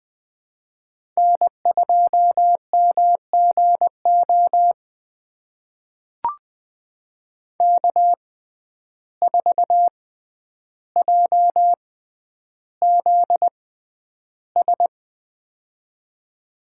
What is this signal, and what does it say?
Morse code, heard as a keyed tone, about 20 wpm, spaced as for 10 wpm, 700 Hz. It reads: N2MGO K4JZS
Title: USA Call Signs - 5 Characters - 6x Character Spacing - 20wpm